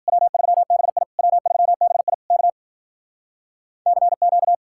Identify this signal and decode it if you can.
Morse code, heard as a keyed tone, about 52 wpm, 700 Hz. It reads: W3BI W3BI K CQ